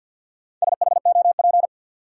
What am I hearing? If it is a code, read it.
Morse code, heard as a keyed tone, about 52 wpm, 700 Hz. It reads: SHOP
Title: Sentences from Top 1600 Words - New Word Drill - 50wpm